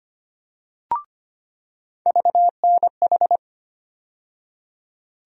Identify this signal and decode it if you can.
Morse code, heard as a keyed tone, about 25 wpm, 700 Hz. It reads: VNH
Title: USA Call Sign Postfixes - 3 Letters 25wpm